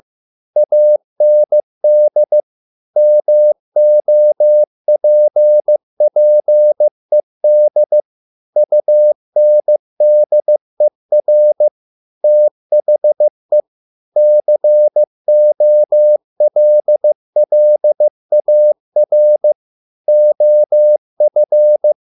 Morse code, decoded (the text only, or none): AND MOPPED UNDER THE COLLAR OF